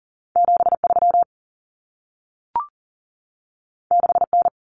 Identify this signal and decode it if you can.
Morse code, heard as a keyed tone, about 40 wpm, 700 Hz. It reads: 73 6N
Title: Common Cut Numbers - 2-Digit 40wpm